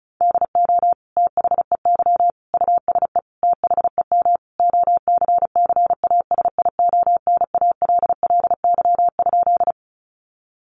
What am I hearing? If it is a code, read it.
Morse code, heard as a keyed tone, about 34 wpm, 700 Hz. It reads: DO THEY USE THEM OCCASIONALLY?
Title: Sentences from Top 1300 Words 35wpm